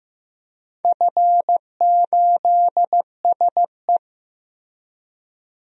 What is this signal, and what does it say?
Morse code, heard as a keyed tone, about 15 wpm, 700 Hz. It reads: F8SE
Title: DXCC-100 Call Signs - 4 Characters - Repeated 2x - 15wpm